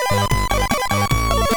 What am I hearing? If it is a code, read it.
There is no code.